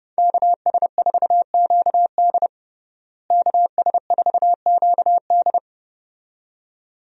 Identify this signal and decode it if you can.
Morse code, heard as a keyed tone, about 32 wpm, 700 Hz. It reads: KS4QD KS4QD
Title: USA Call Signs - 5 Characters - Repeated 2x - 30wpm